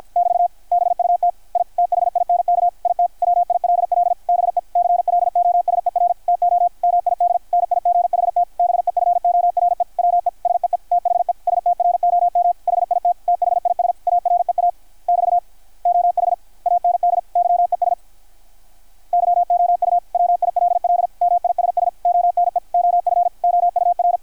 Morse code, heard as a keyed tone, about 53 wpm, 700 Hz. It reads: <BT> BUT I THINK IT WILL BE CLOSER TO MIDNIGHT BEFORE WE SEE THE STROM HIT THIS AREA <BT> OH AND YES YOU WILL MISS ONE OF OUR
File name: archived / CRSnet